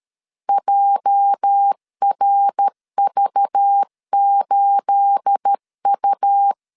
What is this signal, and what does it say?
Morse code, heard as a keyed tone, about 13 wpm, 800 Hz. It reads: JRV8U
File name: CW / DET / 1 9 1500C S B 12wpm 3819